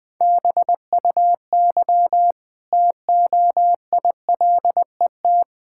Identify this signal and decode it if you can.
Morse code, heard as a keyed tone, about 20 wpm, 700 Hz. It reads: BUY TOILET
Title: Easy Sentences with Hints - 20wpm